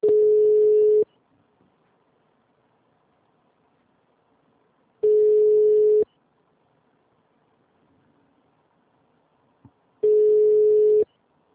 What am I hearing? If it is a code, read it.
Ringback tone (Europe, 425 Hz).
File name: out- -2099- - - .6449